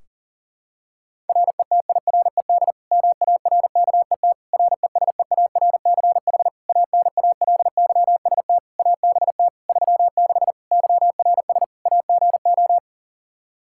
Morse code, heard as a keyed tone, about 40 wpm, 700 Hz. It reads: RETIRED MARKET RESEARCH ANALYST ABT 36 YRS AGO
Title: Practice QSO - Exchange 3 - More Info - 40wpm